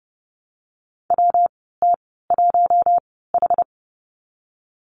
Morse code, heard as a keyed tone, about 31 wpm, spaced as for 17 wpm, 700 Hz. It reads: WT1H
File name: Lesson 19g - Letter w - Call Signs 30wpm